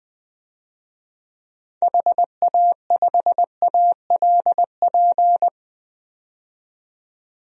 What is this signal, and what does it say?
Morse code, heard as a keyed tone, about 21 wpm, 700 Hz. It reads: HA5ALP
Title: DXCC-100 Call Signs - 6 Characters - 20wpm